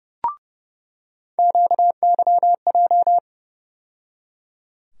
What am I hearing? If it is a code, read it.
Morse code, heard as a keyed tone, about 30 wpm, 700 Hz. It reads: QYJ
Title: USA Call Sign Postfixes - 3 Letters 30wpm